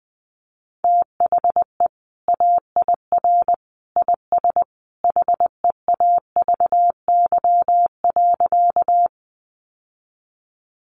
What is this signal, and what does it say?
Morse code, heard as a keyed tone, about 20 wpm, 700 Hz. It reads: THE AIR IS HEAVY.
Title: Sentences from Top 1100 Words 20wpm